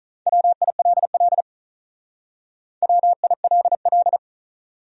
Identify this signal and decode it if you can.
Morse code, heard as a keyed tone, about 44 wpm, 700 Hz. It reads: WILL WILL